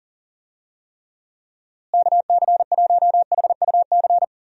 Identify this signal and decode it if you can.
Morse code, heard as a keyed tone, about 40 wpm, 700 Hz. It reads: KC1HUC
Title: USA Call Signs - 6 Characters - Repeated 2x - 40wpm